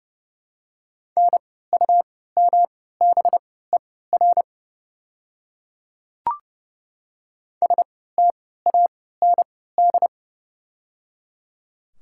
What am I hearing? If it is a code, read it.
Morse code, heard as a keyed tone, about 30 wpm, spaced as for 17 wpm, 700 Hz. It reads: NUMBER STAND
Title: 3x Character Spacing - Top 300 Words - 30wpm